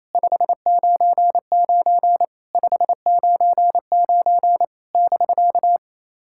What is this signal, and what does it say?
Morse code, heard as a keyed tone, about 28 wpm, 700 Hz. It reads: 599 599 <BK>
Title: SOTA QSO Practice 28wpm